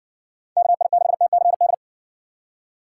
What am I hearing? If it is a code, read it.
Morse code, heard as a keyed tone, about 56 wpm, 700 Hz. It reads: KI6TCD